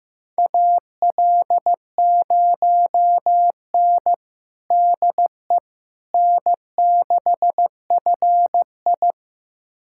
Morse code, heard as a keyed tone, about 15 wpm, 700 Hz. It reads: AL0N DE N6FI
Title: USA Callsign DE Callsign - 4 Characters - 15wpm